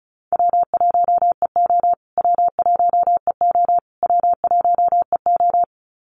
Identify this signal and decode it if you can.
Morse code, heard as a keyed tone, about 34 wpm, 700 Hz. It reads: W1EO W1EO W1EO
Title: USA Call Signs Repeated 3x 35wpm